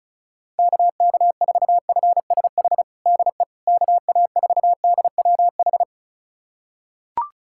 Morse code, heard as a keyed tone, about 34 wpm, 690 Hz. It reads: KK4FSH DE KA4DWH
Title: USA Callsign DE Callsign - 6 Characters - 35wpm